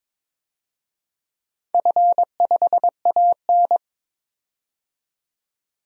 Morse code, heard as a keyed tone, about 22 wpm, 700 Hz. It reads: F5AN
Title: DXCC-100 Call Signs - 4 Characters - 22wpm